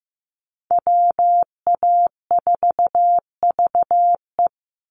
Morse code, heard as a keyed tone, about 15 wpm, 700 Hz. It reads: WA4VE
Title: USA Call Signs 15wpm